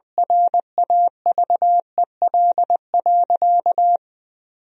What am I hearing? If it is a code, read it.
Morse code, heard as a keyed tone, about 20 wpm, 700 Hz. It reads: RAVEL.
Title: 2x Word-Spacing - Sentences from Top 400 Words - 20wpm